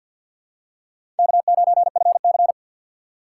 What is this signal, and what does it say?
Morse code, heard as a keyed tone, about 52 wpm, 700 Hz. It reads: K9FC
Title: USA Call Signs - 2x Character Spacing - 50wpm